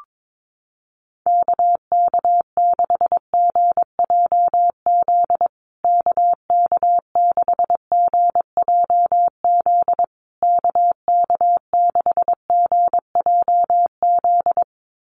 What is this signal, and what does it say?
Morse code, heard as a keyed tone, about 22 wpm, 700 Hz. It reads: KK6GJZ KK6GJZ KK6GJZ
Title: USA Call Signs Repeated 3x 22wpm